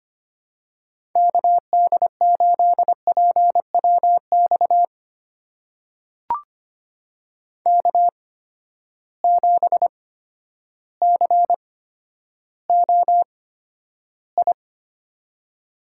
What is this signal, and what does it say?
Morse code, heard as a keyed tone, about 25 wpm, spaced as for 11 wpm, 700 Hz. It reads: KD8PWX K7COI
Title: USA Call Signs - 8x Character Spacing - 25wpm